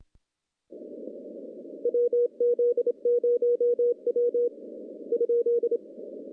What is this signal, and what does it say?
Morse code, heard as a keyed tone, about 26 wpm, 480 Hz. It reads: WZ0W ?
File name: contests / CW-SS-k5tr- -1400